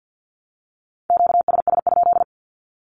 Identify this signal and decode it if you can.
Morse code, heard as a keyed tone, about 48 wpm, 700 Hz. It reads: QSS?
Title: Q-Codes 50wpm